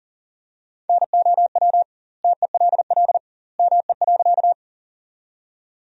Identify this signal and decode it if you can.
Morse code, heard as a keyed tone, about 40 wpm, 700 Hz. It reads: NOW TELL ME.
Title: 2x Word-Spacing - Sentences from Top 200 Words - 40wpm